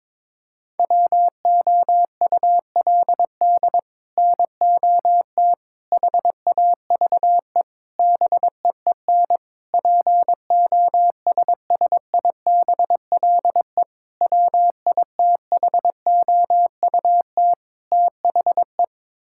Morse code, heard as a keyed tone, about 22 wpm, 700 Hz. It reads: WOULD NOT HAVE BEEN POSSIBLE WITHOUT THE
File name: practice / 22wpm / Sets of 7 Words - 22wpm